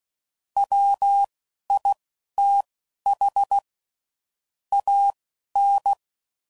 Morse code, heard as a keyed tone, about 16 wpm, spaced as for 12 wpm, 800 Hz. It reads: WITH AN
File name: CW Code Groups @10WPM / Technical Subjects / SwPwrSup